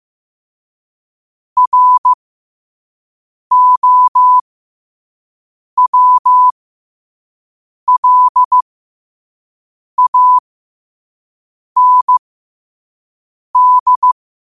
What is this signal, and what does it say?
Morse code, heard as a keyed tone, about 15 wpm, spaced as for 5 wpm, 1000 Hz. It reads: ROWLAND
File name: qotdfiles / 5wpm